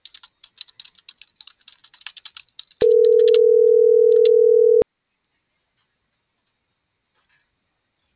Ringback tone (North America), 440 + 480 Hz.